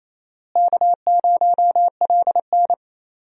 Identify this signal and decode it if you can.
Morse code, heard as a keyed tone, about 28 wpm, 700 Hz. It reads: K0LN
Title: USA Call Signs - 4 Characters - Repeated 3x - 28wpm